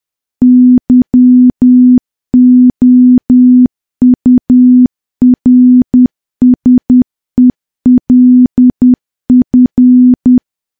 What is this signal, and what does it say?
Morse code, heard as a keyed tone, about 10 wpm, 250 Hz. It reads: YOURSELF
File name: morse / morse-training-10WPM